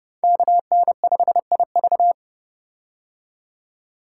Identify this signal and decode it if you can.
Morse code, heard as a keyed tone, about 30 wpm, 700 Hz. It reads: KN5IV
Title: Code Groups - 5 Letter:Numbers - 30wpm